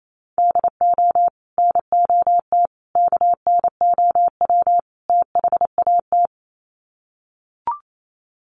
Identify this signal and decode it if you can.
Morse code, heard as a keyed tone, about 28 wpm, 700 Hz. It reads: DO NOT KNOW THAT
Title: Sets of 4 Words 28wpm